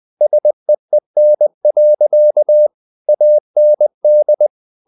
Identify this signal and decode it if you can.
Morse code, heard as a keyed tone, about 20 wpm, 600 Hz. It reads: SEEN. AND
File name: the old man and the sea part 2 20wpm